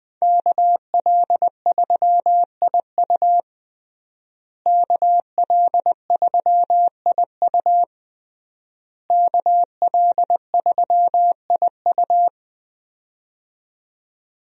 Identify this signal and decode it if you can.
Morse code, heard as a keyed tone, about 21 wpm, 700 Hz. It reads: KL3IU KL3IU KL3IU
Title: USA Call Signs - 5 Characters - Repeated 3x - 20wpm